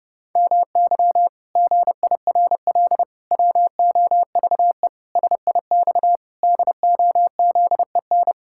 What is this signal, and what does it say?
Morse code, heard as a keyed tone, about 32 wpm, 700 Hz. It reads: MY GIRL WOVE SIX DOZEN
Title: Pangram Challenge v2 - 30wpm